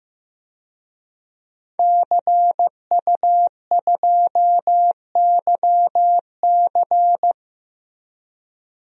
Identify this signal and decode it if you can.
Morse code, heard as a keyed tone, about 15 wpm, 700 Hz. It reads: CU2YC
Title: DXCC-100 Call Signs - 5 Characters - Repeated 2x - 15wpm